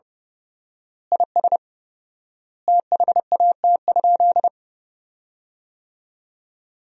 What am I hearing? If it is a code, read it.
Morse code, heard as a keyed tone, about 30 wpm, 700 Hz. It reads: IS THAT?
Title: 4x Word-Spacing - Sentences from Top 200 Words - 30wpm